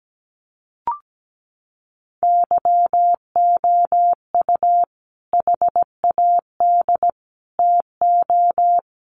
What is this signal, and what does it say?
Morse code, heard as a keyed tone, about 17 wpm, 700 Hz. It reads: YOU HAD TO
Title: Sentences from Top 600 Words 17wpm